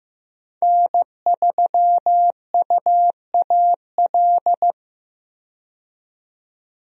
Morse code, heard as a keyed tone, about 15 wpm, 700 Hz. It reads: N3UAL
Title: USA Call Signs - 5 Characters - 15wpm